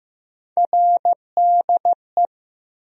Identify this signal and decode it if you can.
Morse code, heard as a keyed tone, about 15 wpm, 700 Hz. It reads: RDE
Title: USA Call Sign Postfixes - 3 Letters 15wpm